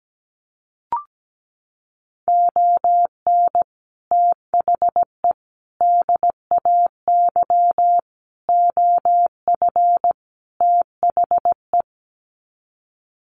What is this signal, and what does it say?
Morse code, heard as a keyed tone, about 17 wpm, 700 Hz. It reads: ON THE DAY OF THE
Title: Sets of 5 Words Encore 17wpm